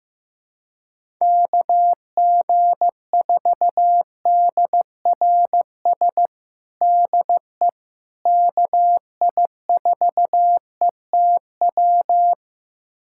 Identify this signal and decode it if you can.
Morse code, heard as a keyed tone, about 15 wpm, 700 Hz. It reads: KG4DRS DE KI4ETW
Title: USA Callsign DE Callsign - 6 Characters - 15wpm